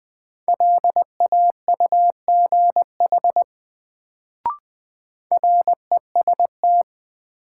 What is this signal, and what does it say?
Morse code, heard as a keyed tone, about 20 wpm, 700 Hz. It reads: LAUGH REST
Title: Sentences from Top 500 Words - New Word Drill - 20wpm